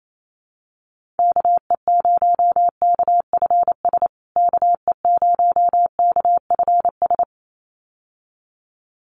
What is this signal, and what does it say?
Morse code, heard as a keyed tone, about 28 wpm, 700 Hz. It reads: KE0KFS KE0KFS
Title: USA Call Signs Repeated 2x 28wpm